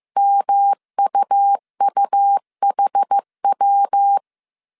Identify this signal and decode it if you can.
Morse code, heard as a keyed tone, about 15 wpm, 800 Hz. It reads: MUUHW